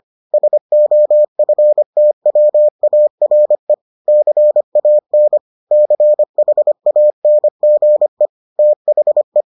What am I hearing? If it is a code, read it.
Morse code, heard as a keyed tone, about 25 wpm, 600 Hz. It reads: SOFTWARE CAN CHANGE THE